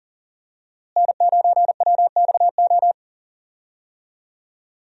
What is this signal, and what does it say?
Morse code, heard as a keyed tone, about 40 wpm, 700 Hz. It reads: N9WXO
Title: USA Call Signs - 5 Characters - Repeated 3x - 40wpm